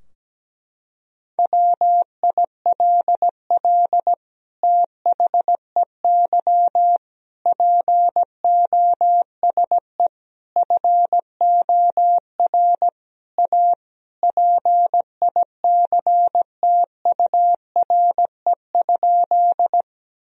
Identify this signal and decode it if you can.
Morse code, heard as a keyed tone, about 17 wpm, 700 Hz. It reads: WILL THEY POSE FOR A PICTURE?
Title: Sentences from Top 1000 Words - Review Part 2 - 17wpm